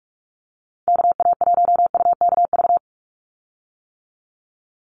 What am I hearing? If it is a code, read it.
Morse code, heard as a keyed tone, about 44 wpm, 690 Hz. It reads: KA1UKV
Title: USA Call Signs 45wpm